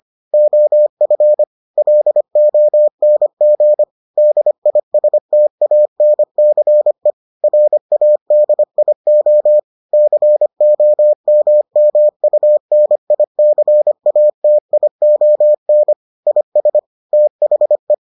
Morse code, heard as a keyed tone, about 25 wpm, 600 Hz. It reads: OF LONG DISTANCE RADIO COMMUNICATION IS THE